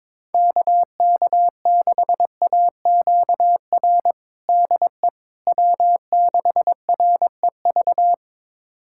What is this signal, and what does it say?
Morse code, heard as a keyed tone, about 22 wpm, 700 Hz. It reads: KK6AQR DE W6REV
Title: USA Callsign DE Callsign 22wpm